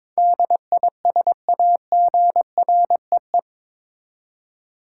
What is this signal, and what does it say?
Morse code, heard as a keyed tone, about 22 wpm, 700 Hz. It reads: DISAGREE
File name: practice / 22wpm / Eight Letter Words - 22wpm